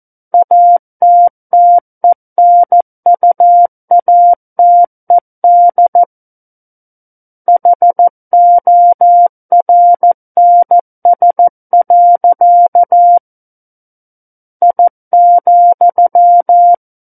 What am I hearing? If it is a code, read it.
Morse code, heard as a keyed tone, about 14 wpm, 700 Hz. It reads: ATTENUATED HORNS. I,
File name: Book One / 05-Chapter 5 - 14 WPM